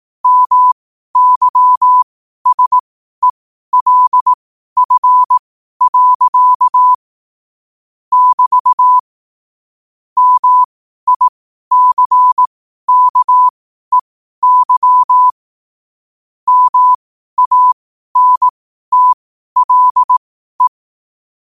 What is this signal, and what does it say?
Morse code, heard as a keyed tone, about 18 wpm, spaced as for 12 wpm, 1000 Hz. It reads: MYSELF. <BT> MICKEY MANTLE
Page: Quotes for Sun, 27 Apr 2025 in Morse Code at 12 words per minute.